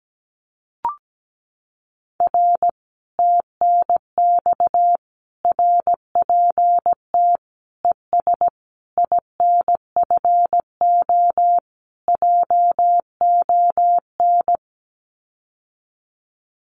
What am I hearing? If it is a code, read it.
Morse code, heard as a keyed tone, about 17 wpm, 700 Hz. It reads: R TNX RPT ES INFO JON
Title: Common QSO Phrases Encore 17wpm